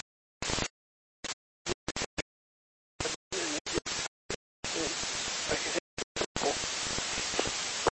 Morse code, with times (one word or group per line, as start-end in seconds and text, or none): none